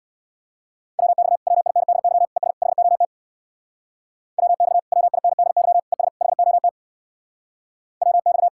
Morse code, heard as a keyed tone, about 64 wpm, 700 Hz. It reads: WX WINDY ES HOT WX WINDY ES HOT WX